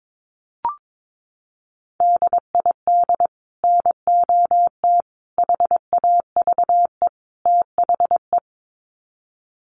Morse code, heard as a keyed tone, about 22 wpm, 700 Hz. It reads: DID NOT HAVE THE
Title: Sets of 4 Words 22wpm